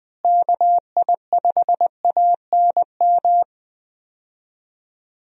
Morse code, heard as a keyed tone, about 20 wpm, 700 Hz. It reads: KI5ANM
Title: USA Call Signs - 6 Characters - Repeated 3x - 20wpm